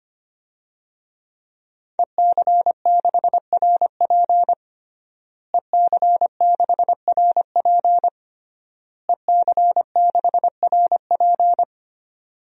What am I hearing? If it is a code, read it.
Morse code, heard as a keyed tone, about 25 wpm, 700 Hz. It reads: EC6RP EC6RP EC6RP
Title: DXCC-100 Call Signs - 5 Characters - Repeated 3x - 25wpm